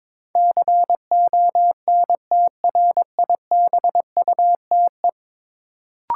Morse code, heard as a keyed tone, about 23 wpm, 700 Hz. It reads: CONTRIBUTE
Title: Sentences from Top 1400 Words - New Word Drill Alt - 22wpm